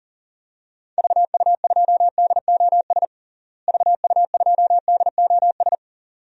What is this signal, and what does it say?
Morse code, heard as a keyed tone, about 37 wpm, 700 Hz. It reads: VU2DOS VU2DOS
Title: DXCC-100 Call Signs - 6 Characters - Repeated 2x - 40wpm